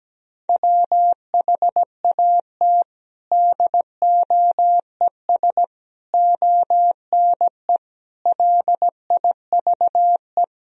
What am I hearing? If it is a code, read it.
Morse code, heard as a keyed tone, about 17 wpm, 700 Hz. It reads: WHAT DOES ONE LIVE
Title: Sentences from Top 200 Words and Abbreviations 17wpm